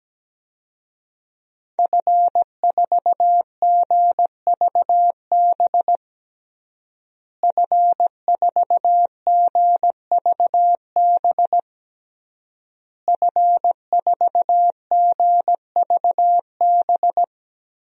Morse code, heard as a keyed tone, about 17 wpm, 700 Hz. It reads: F4GVB F4GVB F4GVB
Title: DXCC-100 Call Signs - 5 Characters - Repeated 3x - 17wpm